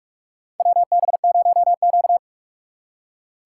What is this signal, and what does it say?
Morse code, heard as a keyed tone, about 46 wpm, 690 Hz. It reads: WB0Q
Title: USA Call Signs - 4x Character Spacing - 45wpm